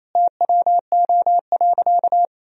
Morse code, heard as a keyed tone, about 28 wpm, 700 Hz. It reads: TWO.
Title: Sentences from Top 1000 Words - Review Part 2 - 28wpm